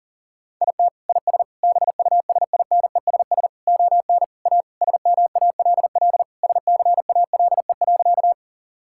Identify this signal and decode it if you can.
Morse code, heard as a keyed tone, about 40 wpm, 700 Hz. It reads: IT IS BUSINESS ON A SMALL SCALE.
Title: Sentences from Top 1000 Words - Review Part 5 - 40wpm